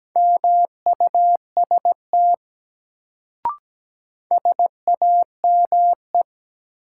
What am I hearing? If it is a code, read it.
Morse code, heard as a keyed tone, about 17 wpm, 700 Hz. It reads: MUST SAME
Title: Sentences from Top 200 Words - New Word Drill - 17wpm